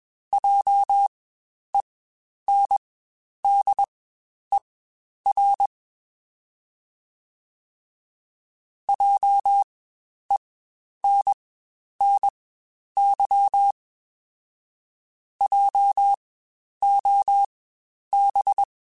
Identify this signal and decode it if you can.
Morse code, heard as a keyed tone, about 21 wpm, spaced as for 10 wpm, 800 Hz. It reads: JENDER JENNY JOB
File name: Spaced Alphas / Spaced Alphas mp3s / Js sp @20